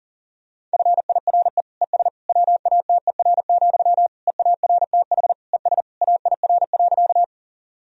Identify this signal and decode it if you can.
Morse code, heard as a keyed tone, about 40 wpm, 700 Hz. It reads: FIRE ES WATER, EARTH ES AIR.
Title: Sentences from Top 400 Words and Abbreviations 40wpm